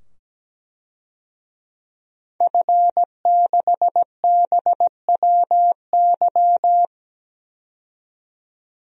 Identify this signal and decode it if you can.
Morse code, heard as a keyed tone, about 17 wpm, 700 Hz. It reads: F6BWY
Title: DXCC-100 Call Signs - 5 Characters - 17wpm